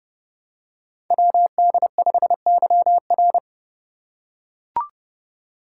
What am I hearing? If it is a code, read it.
Morse code, heard as a keyed tone, about 30 wpm, 700 Hz. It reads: WD5YR
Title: Code Groups - 5 Letter:Numbers - 30wpm